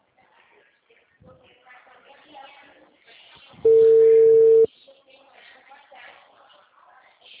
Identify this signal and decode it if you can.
Ringback tone (China, 450 Hz), with speech.